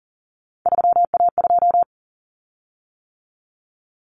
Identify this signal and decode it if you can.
Morse code, heard as a keyed tone, about 40 wpm, 700 Hz. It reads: 3A2
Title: Common Cut Numbers - 3-Digit 40wpm